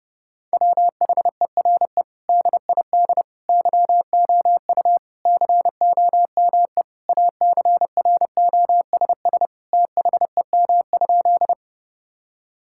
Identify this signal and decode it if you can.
Morse code, heard as a keyed tone, about 30 wpm, 700 Hz. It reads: WHERE DID YOU COME ACROSS THEM?
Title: Sentences from Top 1000 Words - Review Part 5 - 30wpm